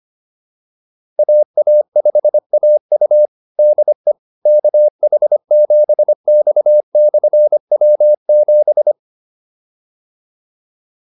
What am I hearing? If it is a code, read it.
Morse code, heard as a keyed tone, about 25 wpm, 600 Hz. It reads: AA5AU DE KH7X/W7